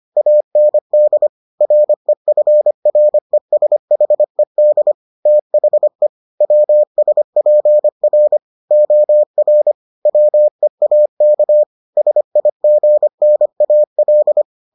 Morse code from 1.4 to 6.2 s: REFRESHED THE

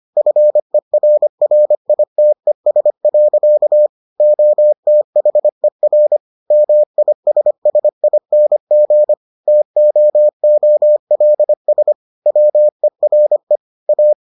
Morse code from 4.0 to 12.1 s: OTHER MISSING TOOLS

